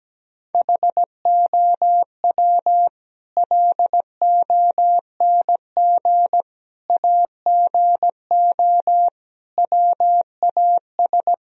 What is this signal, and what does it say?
Morse code, heard as a keyed tone, about 17 wpm, 700 Hz. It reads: HOW LONG AGO WAS
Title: Sentences from Top 1000 Words - Review Part 3 - 17wpm